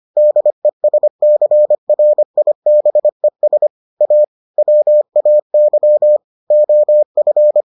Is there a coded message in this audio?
Yes: Morse code from 0.2 to 7.6 s, heard as a keyed tone, about 25 wpm, 600 Hz. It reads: DESCRIBES A WAY OF